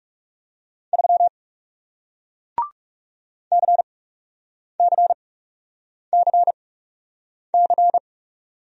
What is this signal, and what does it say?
Morse code, heard as a keyed tone, about 39 wpm, spaced as for 8 wpm, 700 Hz. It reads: 3 CCCC